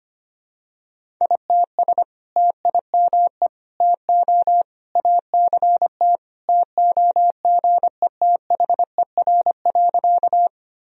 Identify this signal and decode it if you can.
Morse code, heard as a keyed tone, about 25 wpm, 700 Hz. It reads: ITS TIME TO ACT TOGETHER.
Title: Sentences from Top 1000 Words - Review Part 1 - 25wpm